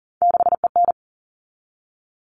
Morse code, heard as a keyed tone, about 40 wpm, 700 Hz. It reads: 6EN
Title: All Cut Numbers - 3-Digit 40wpm